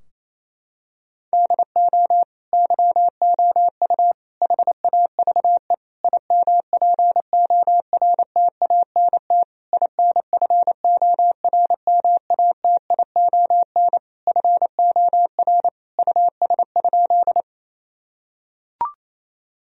Morse code, heard as a keyed tone, about 28 wpm, 700 Hz. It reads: DO YOU HAVE IMPORTANT INFORMATION FOR US?
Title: Sentences from Top 1000 Words - Review Part 3 - 28wpm